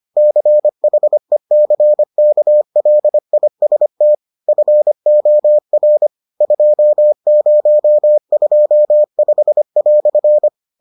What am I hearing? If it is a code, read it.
Morse code, heard as a keyed tone, about 25 wpm, 600 Hz. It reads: CHECKLIST FOR 2025"